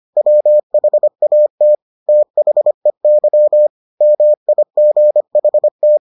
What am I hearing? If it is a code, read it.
Morse code, heard as a keyed tone, about 25 wpm, 600 Hz. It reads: WHAT THEY MIGHT